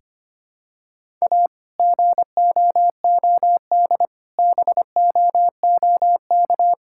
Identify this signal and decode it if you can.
Morse code, heard as a keyed tone, about 25 wpm, 700 Hz. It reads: A GOOD BOOK
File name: practice / 25wpm / Sentences with 2-Word Phrases - 25wpm